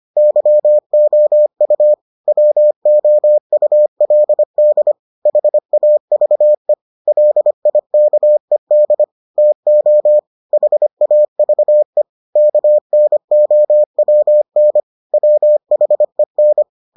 Morse code, heard as a keyed tone, about 25 wpm, 600 Hz. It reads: YOU WOULD HAVE LIKED TO HAVE KNOWN WHEN